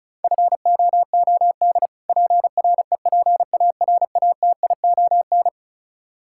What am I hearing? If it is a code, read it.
Morse code, heard as a keyed tone, about 36 wpm, 700 Hz. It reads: FOOD PREPARATION
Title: Practice QSO - Exchange 3 - More Info - 35wpm